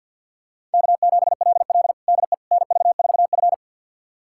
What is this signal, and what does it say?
Morse code, heard as a keyed tone, about 52 wpm, 700 Hz. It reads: K7LL DE NV4F
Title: USA Callsign DE Callsign - 4 Characters - 50wpm